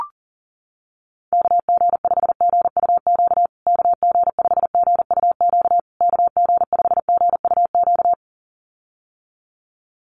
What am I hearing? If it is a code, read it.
Morse code, heard as a keyed tone, about 40 wpm, 700 Hz. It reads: KG5GUQ KG5GUQ KG5GUQ
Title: USA Call Signs Repeated 3x 40wpm